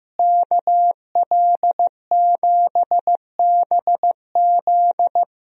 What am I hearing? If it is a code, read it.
Morse code, heard as a keyed tone, about 15 wpm, 700 Hz. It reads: KL7BZ
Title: DXCC-100 Call Signs - 5 Characters - Repeated 2x - 15wpm